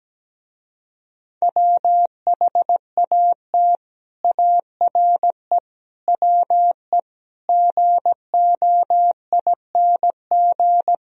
Morse code, heard as a keyed tone, about 17 wpm, 700 Hz. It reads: WHAT ARE WE GOING
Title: Speed-Racing - Sets of 5 Words 17wpm